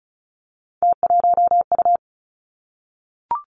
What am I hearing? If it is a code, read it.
Morse code, heard as a keyed tone, about 36 wpm, 700 Hz. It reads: T1U
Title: Three Challenging Letter-Number Combinations 35wpm